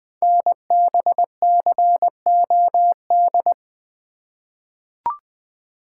Morse code, heard as a keyed tone, about 20 wpm, 700 Hz. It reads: NBCOD
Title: Code Groups - 5 Letters - 20wpm